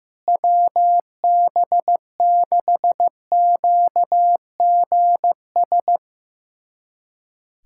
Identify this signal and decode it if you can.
Morse code, heard as a keyed tone, about 15 wpm, 700 Hz. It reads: WB6QGS
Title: USA Callsign DE Callsign - 6 Characters - 15wpm